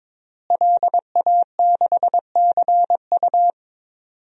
Morse code, heard as a keyed tone, about 22 wpm, 700 Hz. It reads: LA6CU
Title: DXCC-100 Call Signs - 5 Characters - Repeated 3x - 22wpm